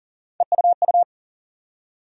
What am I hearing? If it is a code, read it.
Morse code, heard as a keyed tone, about 40 wpm, 700 Hz. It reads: EUU
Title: USA Call Sign Postfixes - 3 Letters 40wpm